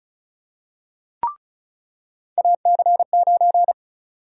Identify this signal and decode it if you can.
Morse code, heard as a keyed tone, about 36 wpm, 700 Hz. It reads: AC9
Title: USA Call Sign Prefixes - 35wpm